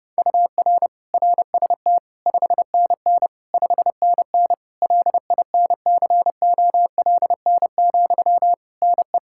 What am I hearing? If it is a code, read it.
Morse code, heard as a keyed tone, about 31 wpm, 700 Hz. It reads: UR RST 5NN 5NN LINCOLN, NE
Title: Practice QSO - Exchange 1 - Beginning - 30wpm